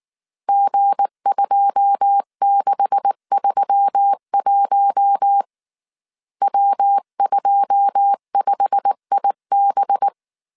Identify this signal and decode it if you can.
Morse code, heard as a keyed tone, about 19 wpm, 800 Hz. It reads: G2631 W25IB
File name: CW / DET / 1 9 1500C S C 18wpm 2539